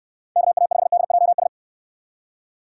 Morse code, heard as a keyed tone, about 69 wpm, 700 Hz. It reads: KA5DJS